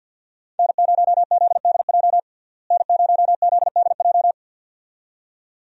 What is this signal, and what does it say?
Morse code, heard as a keyed tone, about 49 wpm, 700 Hz. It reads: N0ZDJ N0ZDJ
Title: USA Call Signs - 5 Characters - Repeated 2x - 50wpm